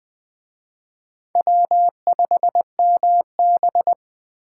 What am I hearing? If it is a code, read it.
Morse code, heard as a keyed tone, about 19 wpm, 700 Hz. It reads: W5MB